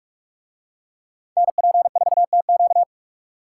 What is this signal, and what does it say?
Morse code, heard as a keyed tone, about 44 wpm, 700 Hz. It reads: NP4TQ